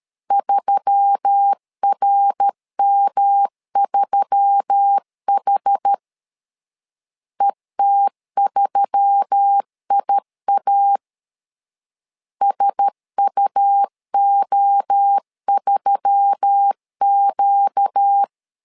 Morse code, heard as a keyed tone, about 13 wpm, 800 Hz. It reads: 3RM3H ET3IA SUO3Q
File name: LEZIONI / 4 Q1S3 12wpm 2023